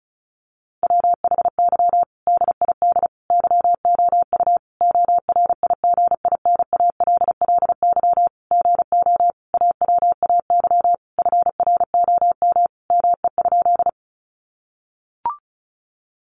Morse code, heard as a keyed tone, about 34 wpm, 700 Hz. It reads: WHY DID YOU ORIGINALLY GO AWAY FROM ME?
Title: Sentences from Top 1500 Words 35wpm